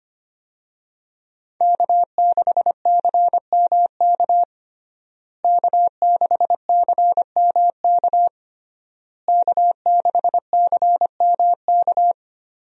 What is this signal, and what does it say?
Morse code, heard as a keyed tone, about 25 wpm, 700 Hz. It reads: K6CMK K6CMK K6CMK
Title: USA Call Signs - 5 Characters - Repeated 3x - 25wpm